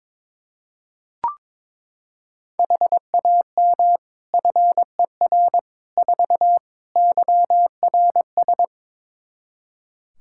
Morse code, heard as a keyed tone, about 22 wpm, 700 Hz. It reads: HAM FER 4 YRS
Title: Practice QSO - Exchange 3 - More Info - 22wpm